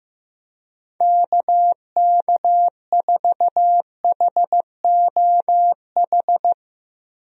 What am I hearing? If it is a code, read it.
Morse code, heard as a keyed tone, about 15 wpm, 700 Hz. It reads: KK4HOH
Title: USA Call Signs - 6 Characters - 15wpm